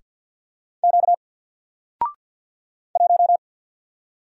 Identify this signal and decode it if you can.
Morse code, heard as a keyed tone, about 48 wpm, 700 Hz. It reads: Q 1
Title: Single Letter-Number Alt 50wpm